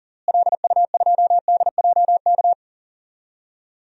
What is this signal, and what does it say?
Morse code, heard as a keyed tone, about 37 wpm, 700 Hz. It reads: LU2DJK
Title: DXCC-100 Call Signs - 6 Characters - Repeated 3x - 40wpm